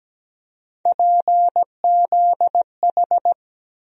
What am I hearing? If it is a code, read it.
Morse code, heard as a keyed tone, about 17 wpm, 700 Hz. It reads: PZH
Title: USA Call Sign Postfixes - 3 Letters 17wpm